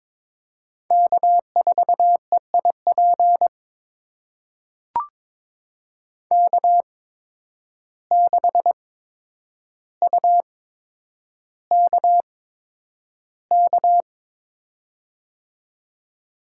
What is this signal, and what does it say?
Morse code, heard as a keyed tone, about 22 wpm, spaced as for 9 wpm, 700 Hz. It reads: K4EIP K6UKK
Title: USA Call Signs - 5 Characters - 8x Character Spacing - 22wpm